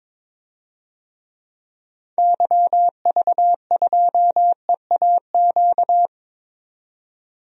Morse code, heard as a keyed tone, about 22 wpm, 700 Hz. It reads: YV2EAQ
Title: DXCC-100 Call Signs - 6 Characters - 22wpm